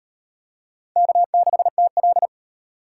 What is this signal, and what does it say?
Morse code, heard as a keyed tone, about 38 wpm, 700 Hz. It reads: K6TL